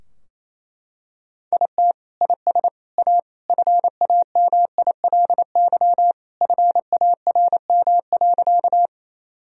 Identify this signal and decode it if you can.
Morse code, heard as a keyed tone, about 27 wpm, 700 Hz. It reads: IT IS A FAMILY FARM.
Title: Sentences from Top 1000 Words - Review Part 2 - 28wpm